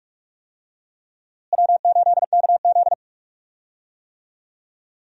Morse code, heard as a keyed tone, about 46 wpm, 700 Hz. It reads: W8KZ
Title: USA Call Signs - 4 Characters - Repeated 2x - 45wpm